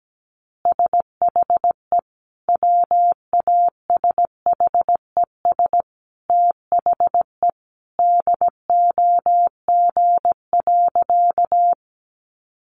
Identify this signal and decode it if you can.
Morse code, heard as a keyed tone, about 17 wpm, 700 Hz. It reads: SHE WASHES THE DOG.
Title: Easy Phrases 17wpm